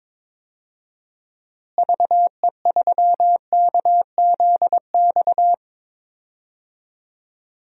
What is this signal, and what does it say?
Morse code, heard as a keyed tone, about 22 wpm, 700 Hz. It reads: VE3KZX
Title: DXCC-100 Call Signs - 6 Characters - Repeated 3x - 22wpm